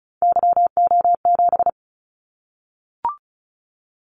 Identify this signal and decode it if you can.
Morse code, heard as a keyed tone, about 36 wpm, 700 Hz. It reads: YO7
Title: Callsign Prefixes for Top 25 DXCC - 35wpm